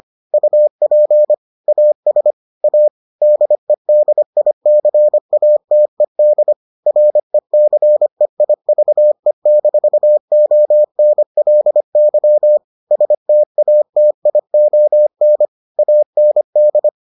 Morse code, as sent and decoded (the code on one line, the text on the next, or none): ..- .--. / .- ... / .- / -.. -.. .. -.-. .- - . -.. / .-. -.-. .. ...- . -....- --- -. .-.. -.-- / ... - .- - .. --- -. / .- -. -..
UP AS A DEDICATED RECEIVE-ONLY STATION AND